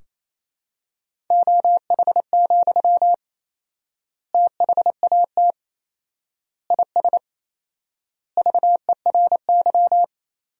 Morse code, heard as a keyed tone, about 27 wpm, 700 Hz. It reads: OH, THAT IS VERY
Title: 4x Word-Spacing - Sentences from Top 500 Words - 28wpm